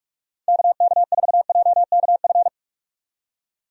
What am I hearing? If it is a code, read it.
Morse code, heard as a keyed tone, about 41 wpm, 700 Hz. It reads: KK4JKF
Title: USA Call Signs - 6 Characters - 45wpm